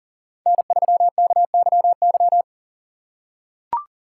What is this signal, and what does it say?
Morse code, heard as a keyed tone, about 40 wpm, 700 Hz. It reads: N3KYY
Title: USA Call Signs - 5 Characters - 3x Character Spacing - 40wpm